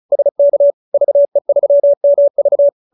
Morse code, heard as a keyed tone, about 34 wpm, 570 Hz. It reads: SK VE3MV